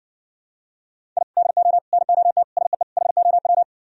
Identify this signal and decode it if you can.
Morse code, heard as a keyed tone, about 60 wpm, 700 Hz. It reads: I DO NOT SEE HOW